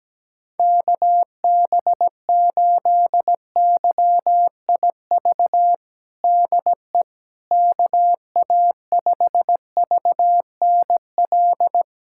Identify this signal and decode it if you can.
Morse code, heard as a keyed tone, about 17 wpm, 700 Hz. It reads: KB8YIV DE KA5VNL
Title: USA Callsign DE Callsign - 6 Characters - 17wpm